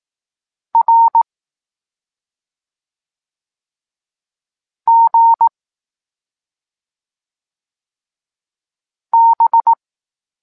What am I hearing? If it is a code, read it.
Morse code, heard as a keyed tone, about 18 wpm, 920 Hz. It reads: R G B
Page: Tekst 23.3 – Ciąg dalszy doskonalenia odbioru poznanych liter